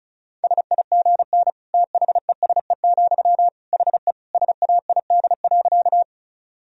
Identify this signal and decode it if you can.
Morse code, heard as a keyed tone, about 36 wpm, 700 Hz. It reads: SIGN THESE, HE SAID.
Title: Sentences from Top 1000 Words - Review Part 1 - 35wpm